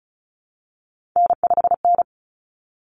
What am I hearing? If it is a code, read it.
Morse code, heard as a keyed tone, about 34 wpm, 700 Hz. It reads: N5N
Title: Common Cut Numbers - 3-Digit 35wpm